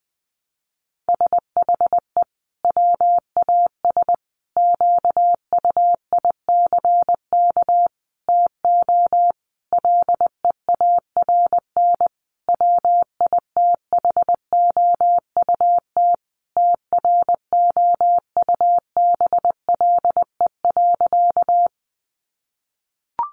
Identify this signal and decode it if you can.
Morse code, heard as a keyed tone, about 20 wpm, 700 Hz. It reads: SHE WAS QUICK TO LEARN WITHOUT TROUBLE.
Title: Sentences from Top 1200 Words 20wpm